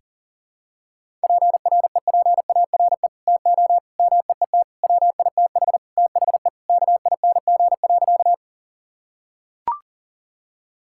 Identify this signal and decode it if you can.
Morse code, heard as a keyed tone, about 40 wpm, 700 Hz. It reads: PREPARE TO MEET WITH THE KING.
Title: Sentences from Top 1000 Words - Review Part 6 - 40wpm